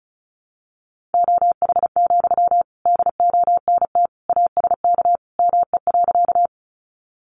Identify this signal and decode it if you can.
Morse code, heard as a keyed tone, about 34 wpm, 700 Hz. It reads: OH, DONT ASK ME.
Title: Sentences from Top 500 Words 35wpm